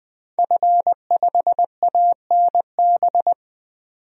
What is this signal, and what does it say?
Morse code, heard as a keyed tone, about 20 wpm, 700 Hz. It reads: F5ANB
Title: DXCC-100 Call Signs - 5 Characters - 20wpm